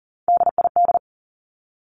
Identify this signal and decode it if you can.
Morse code, heard as a keyed tone, about 40 wpm, 700 Hz. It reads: DID
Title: Top 200 Words 40wpm